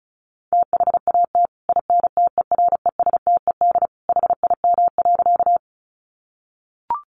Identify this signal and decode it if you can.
Morse code, heard as a keyed tone, about 34 wpm, 700 Hz. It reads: THAT INTERESTED HIM.
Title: Sentences from Top 900 Words 35wpm